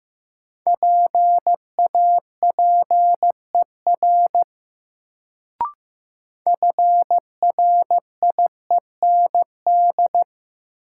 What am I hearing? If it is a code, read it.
Morse code, heard as a keyed tone, about 15 wpm, 700 Hz. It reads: PAPER FRIEND
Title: Sentences from Top 300 Words - New Word Drill - 15wpm